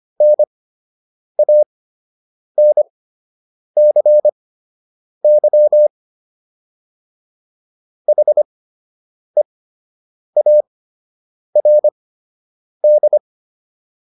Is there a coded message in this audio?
Yes: Morse code from 0.2 to 13.2 s, heard as a keyed tone, about 25 wpm, spaced as for 8 wpm, 600 Hz. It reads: NANCY HEARD